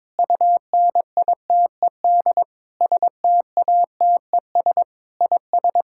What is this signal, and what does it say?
Morse code, heard as a keyed tone, about 21 wpm, 700 Hz. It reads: UNITED STATES IS
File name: practice / 22wpm / Sentences with 8-Word Phrases - 22wpm